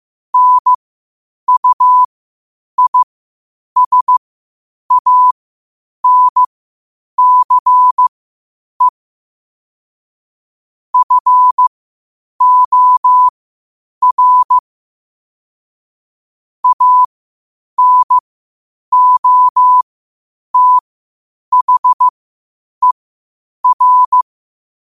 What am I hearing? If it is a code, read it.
Morse code, heard as a keyed tone, about 15 wpm, spaced as for 8 wpm, 1000 Hz. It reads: NUISANCE FOR ANOTHER
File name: qotdfiles / 8wpm